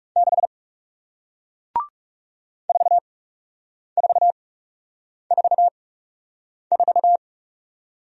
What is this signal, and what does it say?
Morse code, heard as a keyed tone, about 41 wpm, spaced as for 8 wpm, 700 Hz. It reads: 6 4444